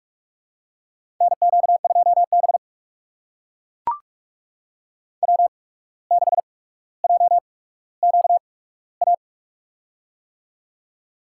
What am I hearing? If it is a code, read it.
Morse code, heard as a keyed tone, about 43 wpm, spaced as for 18 wpm, 700 Hz. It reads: NQ2B W6JQA
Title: USA Call Signs - 8x Character Spacing - 45wpm